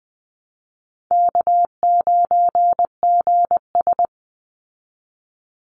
Morse code, heard as a keyed tone, about 20 wpm, 700 Hz. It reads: K9GS
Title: Call Signs Repeated 3x - 20wpm